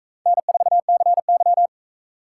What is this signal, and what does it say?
Morse code, heard as a keyed tone, about 40 wpm, 700 Hz. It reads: N4CY